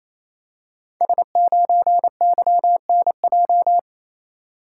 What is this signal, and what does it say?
Morse code, heard as a keyed tone, about 28 wpm, 700 Hz. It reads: S9YNJ
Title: Code Groups - 5 Letter:Numbers - 28wpm